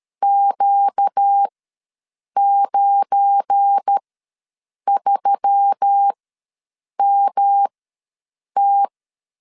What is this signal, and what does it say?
Morse code, heard as a keyed tone, about 13 wpm, spaced as for 7 wpm, 800 Hz. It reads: Q93MT
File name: LEZIONI / 8 YJ45 / 8 YJ ws 12wpm 3342